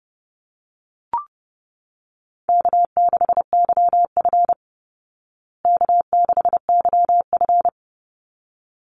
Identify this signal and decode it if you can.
Morse code, heard as a keyed tone, about 30 wpm, 700 Hz. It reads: K6YF K6YF
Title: Easy Call Signs Repeated 2x - 30wpm